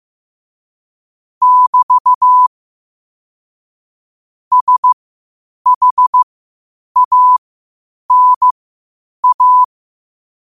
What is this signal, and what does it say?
Morse code, heard as a keyed tone, about 15 wpm, spaced as for 8 wpm, 1000 Hz. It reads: <BT> SHANA
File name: qotdfiles / 8wpm